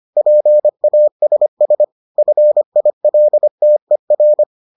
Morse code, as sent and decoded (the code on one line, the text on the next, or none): .--. .- ... ... / ..-. .. .-.. - . .-.
PASS FILTER